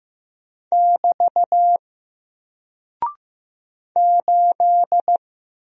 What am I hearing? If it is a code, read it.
Morse code, heard as a keyed tone, about 15 wpm, 700 Hz. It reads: <BT> 8
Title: Single Character (Letter:Number:Punctuation:Prosign) 15wpm